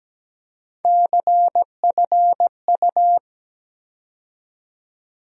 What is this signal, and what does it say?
Morse code, heard as a keyed tone, about 17 wpm, 700 Hz. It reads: CFU
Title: USA Call Sign Postfixes - 3 Letters 17wpm